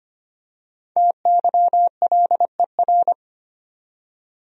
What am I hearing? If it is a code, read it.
Morse code, heard as a keyed tone, about 25 wpm, 700 Hz. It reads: TYLER
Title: USA Common Names - 25wpm